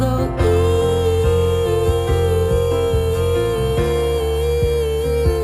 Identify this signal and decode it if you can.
Busy tone (China, 450 Hz), with noise.